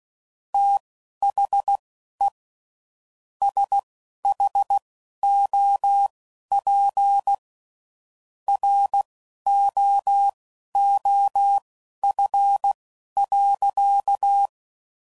Morse code, heard as a keyed tone, about 16 wpm, spaced as for 12 wpm, 800 Hz. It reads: THE SHOP ROOF.
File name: CW Code Groups @10WPM / Weather Sets / Weather 4 24